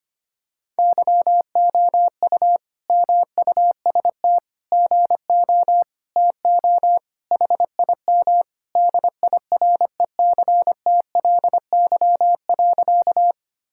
Morse code, heard as a keyed tone, about 25 wpm, 700 Hz. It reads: YOU MUST GO TO HIM DIRECTLY.
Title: Sentences from Top 1000 Words - Review Part 5 - 25wpm